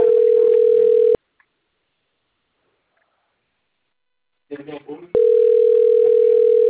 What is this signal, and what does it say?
Ringback tone (North America), 440 + 480 Hz, with speech.
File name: rg-1600-unknown- - - .63134